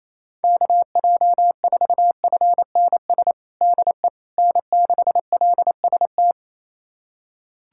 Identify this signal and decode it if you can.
Morse code, heard as a keyed tone, about 27 wpm, 700 Hz. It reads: KJ4FNS DE N6LST
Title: USA Callsign DE Callsign 28wpm